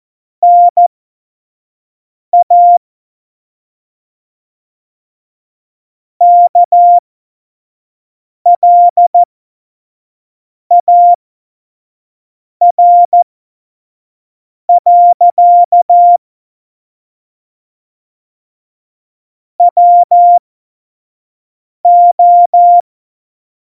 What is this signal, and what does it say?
Morse code, heard as a keyed tone, about 14 wpm, spaced as for 5 wpm, 700 Hz. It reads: NA KLAR. WO